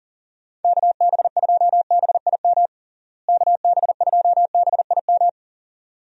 Morse code, heard as a keyed tone, about 40 wpm, 700 Hz. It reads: KB2BIM KB2BIM
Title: USA Call Signs - 6 Characters - Repeated 2x - 40wpm